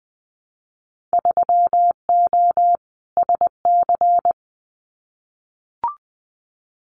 Morse code, heard as a keyed tone, about 20 wpm, 700 Hz. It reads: 3O SC
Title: Winter Field Day Exchanges Encore 20wpm